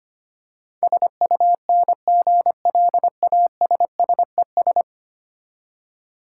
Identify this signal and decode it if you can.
Morse code, heard as a keyed tone, about 25 wpm, 690 Hz. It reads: SUNGLASSES
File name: practice / 25wpm / Easy Words with Hints - 25wpm